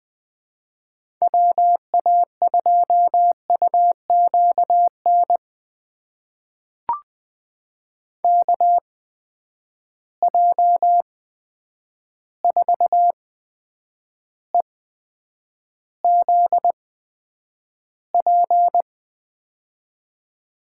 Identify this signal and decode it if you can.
Morse code, heard as a keyed tone, about 21 wpm, spaced as for 9 wpm, 700 Hz. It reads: WA2UQN KJ4EZP
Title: USA Call Signs - 8x Character Spacing - 20wpm